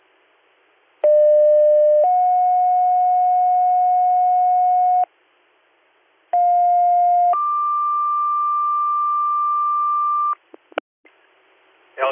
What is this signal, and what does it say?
Two-tone page, heard as a one-second tone, then a three-second tone, about 600 then 730 Hz, about 710 then 1180 Hz. It is heard from 1.0 to 10.3 s.